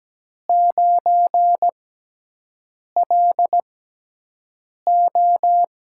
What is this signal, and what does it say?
Morse code, heard as a keyed tone, about 17 wpm, spaced as for 6 wpm, 700 Hz. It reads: 9LO
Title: USA Call Signs - 6 Characters - 6x Character Spacing - 17wpm